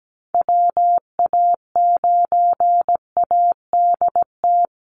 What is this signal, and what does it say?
Morse code, heard as a keyed tone, about 17 wpm, 700 Hz. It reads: WA9ADT
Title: Call Signs Repeated 3x - 17wpm